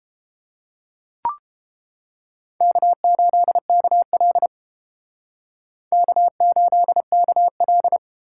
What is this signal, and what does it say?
Morse code, heard as a keyed tone, about 28 wpm, 700 Hz. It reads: K8KL K8KL